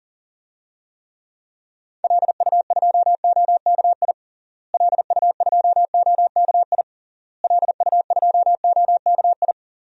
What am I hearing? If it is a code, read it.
Morse code, heard as a keyed tone, about 40 wpm, 700 Hz. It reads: LU2OKI LU2OKI LU2OKI
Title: DXCC-100 Call Signs - 6 Characters - Repeated 3x - 40wpm